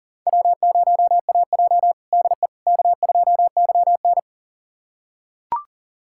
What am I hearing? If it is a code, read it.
Morse code, heard as a keyed tone, about 40 wpm, 700 Hz. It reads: W0AJ DE K2YN
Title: USA Callsign DE Callsign - 4 Characters - 40wpm